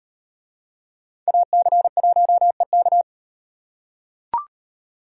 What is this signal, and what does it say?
Morse code, heard as a keyed tone, about 36 wpm, 700 Hz. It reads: AC1EK